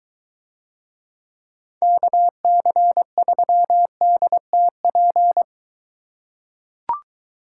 Morse code, heard as a keyed tone, about 23 wpm, 700 Hz. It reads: KC3DTP